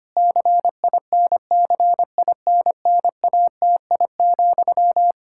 Morse code, heard as a keyed tone, about 25 wpm, 700 Hz. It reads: CINCINNATI,
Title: Practice QSO - Exchange 1 - Beginning - 25wpm